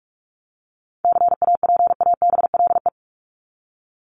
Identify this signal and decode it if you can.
Morse code, heard as a keyed tone, about 44 wpm, 700 Hz. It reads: CAPABLE
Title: Top 1001-1250 Words 45wpm